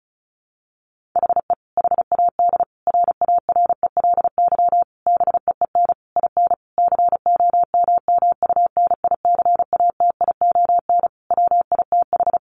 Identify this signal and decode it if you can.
Morse code, heard as a keyed tone, about 34 wpm, 700 Hz. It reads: HE HAD RARELY BEEN IN COMMUNICATION WITH
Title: Sentences from Top 1400 Words 35wpm